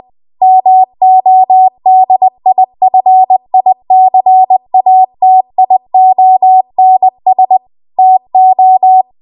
Morse code, heard as a keyed tone, about 20 wpm, 750 Hz. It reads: MODIFICATIONS TO
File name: Archive / 20 WPM / 20WPM